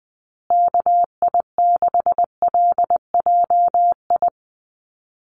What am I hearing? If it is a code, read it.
Morse code, heard as a keyed tone, about 20 wpm, 700 Hz. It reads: KI6LJI
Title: USA Call Signs Repeated 2x 20wpm